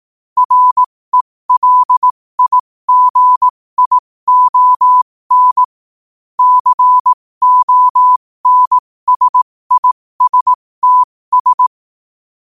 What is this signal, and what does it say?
Morse code, heard as a keyed tone, about 18 wpm, spaced as for 15 wpm, 1000 Hz. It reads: RELIGION CONSISTS
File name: qotdfiles / 15wpm